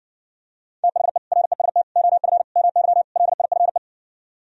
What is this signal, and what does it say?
Morse code, heard as a keyed tone, about 59 wpm, 700 Hz. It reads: THE REST OF MY LIFE